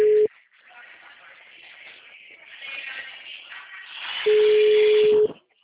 Ringback tone (Europe, 425 Hz).